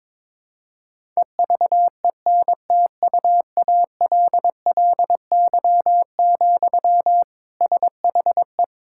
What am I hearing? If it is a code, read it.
Morse code, heard as a keyed tone, about 22 wpm, 700 Hz. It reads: EVENTUALLY, SHE
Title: Sentences from Top 1000 Words - Review Part 6 - 22wpm